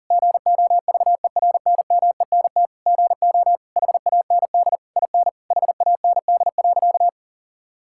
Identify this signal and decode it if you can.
Morse code, heard as a keyed tone, about 40 wpm, 690 Hz. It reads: GOVERNMENT GO HAND IN HAND.
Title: Sentences from Top 1000 Words - Review Part 5 - 40wpm